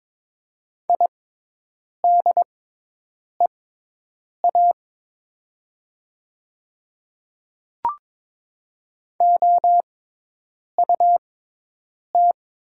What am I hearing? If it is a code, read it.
Morse code, heard as a keyed tone, about 23 wpm, spaced as for 8 wpm, 700 Hz. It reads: IDEA OUT
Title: 6x Character Spacing - Top 500 Words - 22wpm